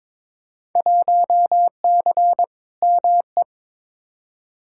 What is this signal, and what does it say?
Morse code, heard as a keyed tone, about 22 wpm, 700 Hz. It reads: 1C ME
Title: Field Day Exchanges - 22wpm